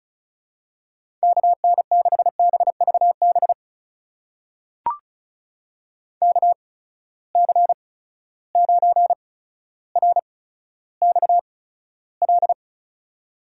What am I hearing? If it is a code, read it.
Morse code, heard as a keyed tone, about 36 wpm, spaced as for 15 wpm, 700 Hz. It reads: KN6BVB KC9RXL
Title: USA Call Signs - 6 Characters - 8x Character Spacing - 35wpm